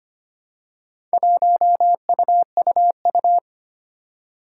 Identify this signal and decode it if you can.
Morse code, heard as a keyed tone, about 25 wpm, 700 Hz. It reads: 1UUU
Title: Four Challenging Letter-Number Combinations 25wpm